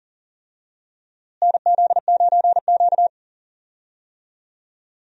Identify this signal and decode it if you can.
Morse code, heard as a keyed tone, about 40 wpm, 700 Hz. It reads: NZ9Q
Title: USA Call Signs - 4 Characters - Repeated 3x - 40wpm